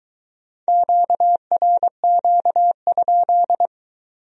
Morse code, heard as a keyed tone, about 24 wpm, 700 Hz. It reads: QRQ?